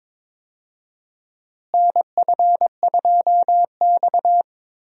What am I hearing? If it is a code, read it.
Morse code, heard as a keyed tone, about 22 wpm, 700 Hz. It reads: NF2X
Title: USA Call Signs - 4 Characters - Repeated 2x - 22wpm